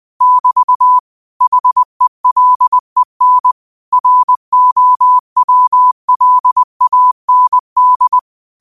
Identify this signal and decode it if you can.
Morse code, heard as a keyed tone, about 20 wpm, 1000 Hz. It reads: <BT> HELEN ROWLAND
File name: qotdfiles / 20wpm